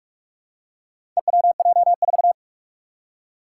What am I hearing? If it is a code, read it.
Morse code, heard as a keyed tone, about 43 wpm, 700 Hz. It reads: EWJ4
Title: Four Challenging Letter-Number Combinations 45wpm